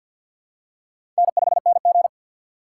Morse code, heard as a keyed tone, about 52 wpm, 700 Hz. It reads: N5NG
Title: USA Call Signs - 4x Character Spacing - 50wpm